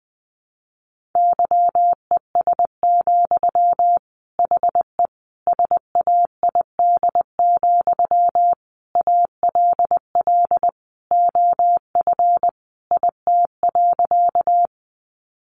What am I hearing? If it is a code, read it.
Morse code, heard as a keyed tone, about 20 wpm, 700 Hz. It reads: YES, HE SAID, ALL OF IT.
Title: Sentences from Top 500 Words 20wpm